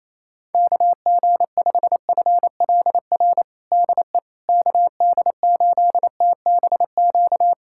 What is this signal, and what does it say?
Morse code, heard as a keyed tone, about 28 wpm, 700 Hz. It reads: KG5FLR DE KD8TBQ
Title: USA Callsign DE Callsign 28wpm